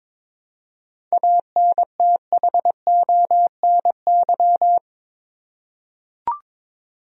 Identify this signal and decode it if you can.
Morse code, heard as a keyed tone, about 21 wpm, 700 Hz. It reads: ANTHONY
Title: USA Common Names - 22wpm